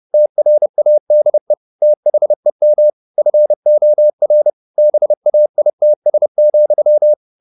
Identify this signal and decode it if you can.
Morse code, heard as a keyed tone, about 30 wpm, 600 Hz. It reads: TRADE THEM FOR BAITS,
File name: the old man and the sea / the old man and the sea 1 30wpm